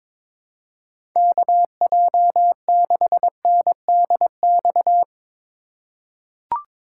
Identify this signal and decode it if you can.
Morse code, heard as a keyed tone, about 22 wpm, 700 Hz. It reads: KJ6NDX
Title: USA Call Signs - 6 Characters - 2x Character Spacing - 22wpm